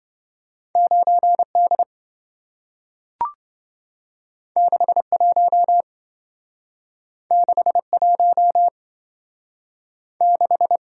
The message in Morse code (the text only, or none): 9D 61 61 6